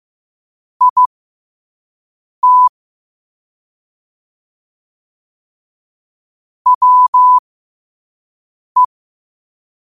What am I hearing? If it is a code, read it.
Morse code, heard as a keyed tone, about 15 wpm, spaced as for 5 wpm, 1000 Hz. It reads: IT WE
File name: 5wpm